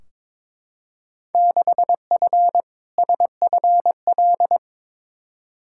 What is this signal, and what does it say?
Morse code, heard as a keyed tone, about 23 wpm, 700 Hz. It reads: 6F SFL
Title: Field Day Exchanges Encore - 22wpm